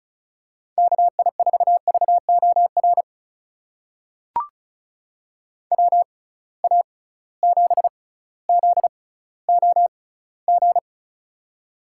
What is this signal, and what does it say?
Morse code, heard as a keyed tone, about 36 wpm, spaced as for 18 wpm, 700 Hz. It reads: KI4VOR WA7ZOG
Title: USA Call Signs - 6 Characters - 6x Character Spacing - 35wpm